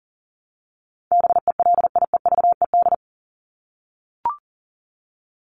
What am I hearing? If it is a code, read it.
Morse code, heard as a keyed tone, about 40 wpm, 700 Hz. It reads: BELIEVED
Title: ED Suffix - Top 100 Words with Base Words 40wpm